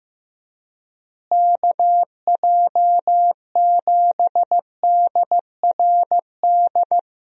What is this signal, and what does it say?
Morse code, heard as a keyed tone, about 16 wpm, 700 Hz. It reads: KJ7DRD
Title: USA Call Signs - 6 Characters - Repeated 2x - 15wpm